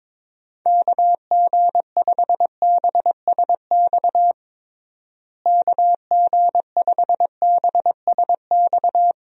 Morse code, heard as a keyed tone, about 22 wpm, 700 Hz. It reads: KG5BSX KG5BSX
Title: USA Call Signs - 6 Characters - Repeated 3x - 22wpm